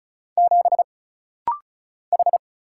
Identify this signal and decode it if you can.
Morse code, heard as a keyed tone, about 34 wpm, 700 Hz. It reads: 7 H
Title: Single Letter-Number Alt - Rapid-Fire 35wpm